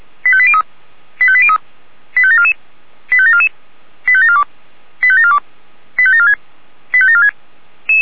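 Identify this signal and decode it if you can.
Five-tone selective call (ZVEI), heard as quick runs of five tones, tones of about 80 ms. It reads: 85803 85803 85744 85744 85742 85742 85746 85746
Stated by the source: ZVEI five-tone call received by SDR.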